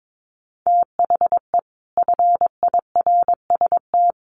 Morse code, heard as a keyed tone, about 22 wpm, 700 Hz. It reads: THE FIRST
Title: Sets of 5 Words 22wpm